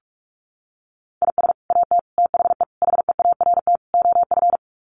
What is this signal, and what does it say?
Morse code, heard as a keyed tone, about 44 wpm, 700 Hz. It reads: IS AT THE HEART OF
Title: Sets of 5 Words Encore2 45wpm